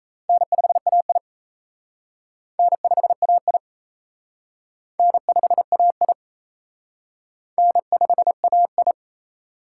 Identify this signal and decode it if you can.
Morse code, heard as a keyed tone, about 33 wpm, 700 Hz. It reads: N5AI N5AI N5AI N5AI
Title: Speed Racing - USA 1x2 Call Signs 28wpm